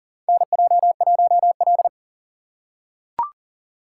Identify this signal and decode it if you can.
Morse code, heard as a keyed tone, about 40 wpm, 700 Hz. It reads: NJ1L
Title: USA Call Signs - 4 Characters - 4x Character Spacing - 40wpm